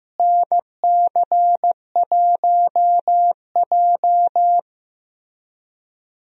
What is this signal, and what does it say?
Morse code, heard as a keyed tone, about 15 wpm, 700 Hz. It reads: NC1J
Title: USA Call Signs - 4 Characters - Repeated 3x - 15wpm